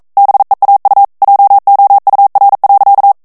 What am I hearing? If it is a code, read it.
Morse code, heard as a keyed tone, about 42 wpm, 800 Hz. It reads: BEAU JOUR.
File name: FRANCAIS TEXTE CLAIR 1 VITESSE 40 MOTS MINUTES